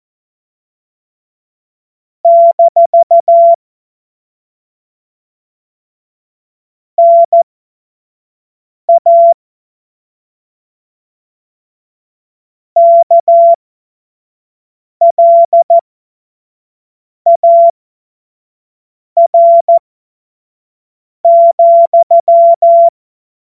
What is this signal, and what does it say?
Morse code, heard as a keyed tone, about 14 wpm, spaced as for 5 wpm, 680 Hz. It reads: - NA KLAR,
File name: Morsekurs / Witze / Witz 21. September 2021, 25 F70